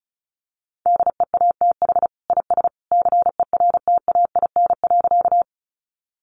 Morse code, heard as a keyed tone, about 34 wpm, 700 Hz. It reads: DEATH IS CERTAIN.
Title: Sentences from Top 1100 Words 35wpm